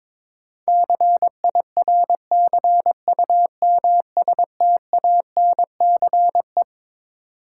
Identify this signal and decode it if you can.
Morse code, heard as a keyed tone, about 22 wpm, 700 Hz. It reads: CIRCUMSTANCE
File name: Twelve Letter Words - 22wpm